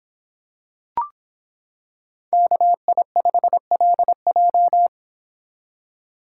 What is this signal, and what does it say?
Morse code, heard as a keyed tone, about 26 wpm, 700 Hz. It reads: KI5LJ